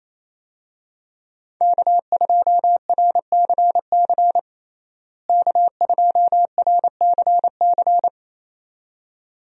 Morse code, heard as a keyed tone, about 28 wpm, 700 Hz. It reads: K2RCC K2RCC
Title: USA Call Signs - 5 Characters - Repeated 2x - 28wpm